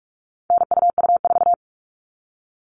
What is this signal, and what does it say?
Morse code, heard as a keyed tone, about 44 wpm, 700 Hz. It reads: NUU4
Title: All Cut Numbers - 4-Digit 45wpm